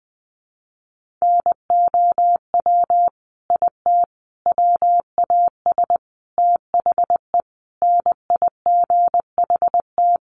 Morse code, heard as a keyed tone, about 20 wpm, 700 Hz. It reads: NOW IT WAS THE NIGHT
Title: Sentences from Top 1400 Words 20wpm